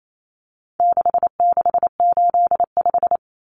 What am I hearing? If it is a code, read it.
Morse code, heard as a keyed tone, about 28 wpm, 700 Hz. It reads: 6685
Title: Common Cut Numbers - 4-Digit 28wpm